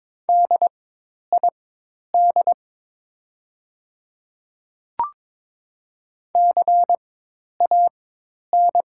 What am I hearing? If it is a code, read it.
Morse code, heard as a keyed tone, about 22 wpm, spaced as for 10 wpm, 700 Hz. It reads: DID CAN
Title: 4x Character Spacing - Top 300 Words - 22wpm